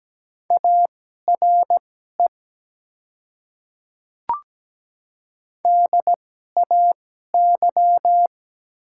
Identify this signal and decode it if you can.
Morse code, heard as a keyed tone, about 17 wpm, spaced as for 12 wpm, 700 Hz. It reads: ARE DAY
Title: 2x Character Spacing - Top 100 Words - 17wpm